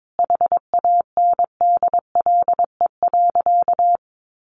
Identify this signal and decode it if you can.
Morse code, heard as a keyed tone, about 22 wpm, 700 Hz. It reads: HANDLE.
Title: Sentences from Top 1100 Words 22wpm